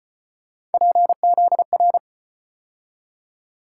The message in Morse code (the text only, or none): PZR